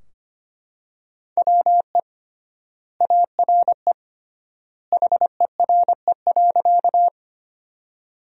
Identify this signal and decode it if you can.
Morse code, heard as a keyed tone, about 25 wpm, 700 Hz. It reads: WE ARE HERE.
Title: 3x Word-Spacing - Sentences from Top 200 Words - 25wpm